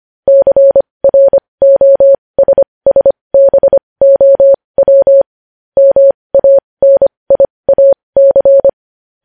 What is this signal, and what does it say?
Morse code, heard as a keyed tone, about 25 wpm, spaced as for 20 wpm, 550 Hz. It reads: CROSSBOW MANIAC
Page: news-headlines-in-morse-code-at-20-wpm